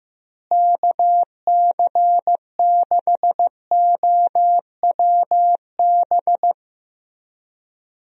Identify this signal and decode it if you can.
Morse code, heard as a keyed tone, about 15 wpm, 700 Hz. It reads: KC6OWB
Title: USA Call Signs - 6 Characters - Repeated 3x - 15wpm